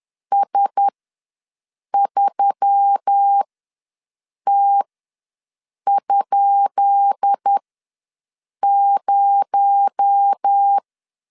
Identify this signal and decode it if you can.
Morse code, heard as a keyed tone, about 10 wpm, spaced as for 6 wpm, 800 Hz. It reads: S3T?0
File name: LEZIONI / 10 SIGNS / 10B INT ws 10wpm 4134